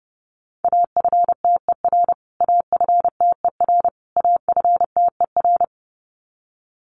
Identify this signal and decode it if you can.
Morse code, heard as a keyed tone, about 30 wpm, 700 Hz. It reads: AFTER AFTER AFTER
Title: Top 300 Words Repeated 30wpm